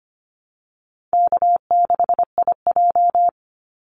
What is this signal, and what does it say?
Morse code, heard as a keyed tone, about 25 wpm, 700 Hz. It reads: K6IJ
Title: Easy Call Signs Repeated 2x - 25wpm